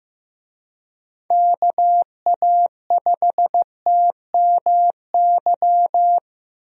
Morse code, heard as a keyed tone, about 16 wpm, 700 Hz. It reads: KA5TMY
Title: USA Call Signs - 6 Characters - Repeated 3x - 15wpm